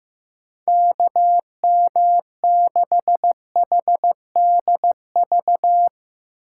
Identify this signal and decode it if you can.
Morse code, heard as a keyed tone, about 16 wpm, 700 Hz. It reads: KM6HDV
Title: Speed Racing - USA 2x3 Call Signs 15wpm